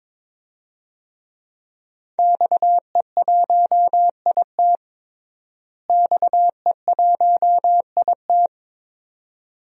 Morse code, heard as a keyed tone, about 22 wpm, 700 Hz. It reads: XE1IT XE1IT
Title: DXCC-100 Call Signs - 5 Characters - Repeated 2x - 22wpm